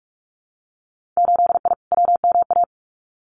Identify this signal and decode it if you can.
Morse code, heard as a keyed tone, about 46 wpm, 700 Hz. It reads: 8I WMA
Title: Winter Field Day Exchanges Encore 45wpm